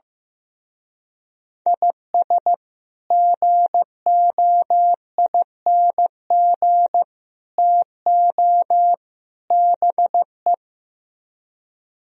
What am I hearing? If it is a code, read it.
Morse code, heard as a keyed tone, about 16 wpm, 700 Hz. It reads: IS GOING TO BE
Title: Speed-Racing - Sets of 4 Words 15wpm